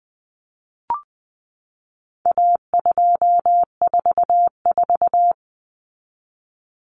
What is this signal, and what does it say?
Morse code, heard as a keyed tone, about 20 wpm, 700 Hz. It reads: A244
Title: Common Cut Numbers - 4-Digit 20wpm